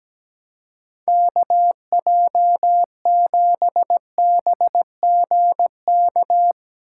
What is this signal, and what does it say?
Morse code, heard as a keyed tone, about 17 wpm, 700 Hz. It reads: KJ7BGK
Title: USA Call Signs - 4x Character Spacing - 17wpm